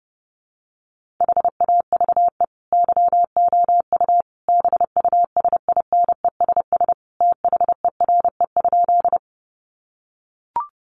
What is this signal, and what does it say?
Morse code, heard as a keyed tone, about 30 wpm, 700 Hz. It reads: HAVE YOU BUSINESS THERE?
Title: Sentences from Top 600 Words 30wpm